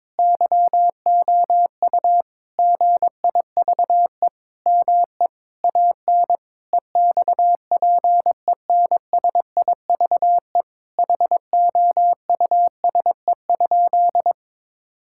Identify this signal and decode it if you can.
Morse code, heard as a keyed tone, about 22 wpm, 700 Hz. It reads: YOU GIVE ME AN EXPENSIVE HOUSE?
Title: Sentences from Top 1000 Words - Review Part 6 - 22wpm